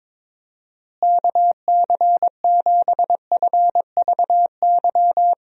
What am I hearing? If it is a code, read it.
Morse code, heard as a keyed tone, about 21 wpm, 700 Hz. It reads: KC7FVY
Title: USA Call Signs - 6 Characters - Repeated 2x - 22wpm